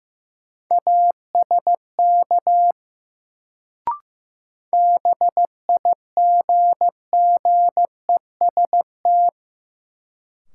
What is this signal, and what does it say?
Morse code, heard as a keyed tone, about 15 wpm, 700 Hz. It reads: ASK BIGGEST
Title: Sets of 4 Words - New Word Drill - 15wpm